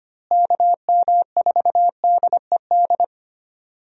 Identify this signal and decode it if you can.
Morse code, heard as a keyed tone, about 25 wpm, 700 Hz. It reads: KM4DED
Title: USA Call Signs - 6 Characters - Repeated 3x - 25wpm